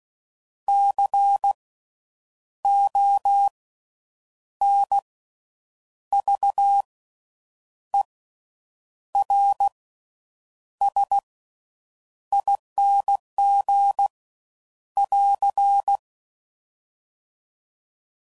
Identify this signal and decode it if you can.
Morse code, heard as a keyed tone, about 16 wpm, spaced as for 7 wpm, 800 Hz. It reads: CONVERSING <AR>
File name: Suffixes / Special INGs / Slow With Faster Ings